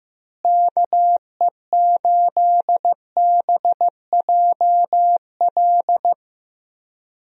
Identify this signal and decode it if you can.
Morse code, heard as a keyed tone, about 15 wpm, 700 Hz. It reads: KE8BJL
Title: USA Call Signs - 6 Characters - Repeated 2x - 15wpm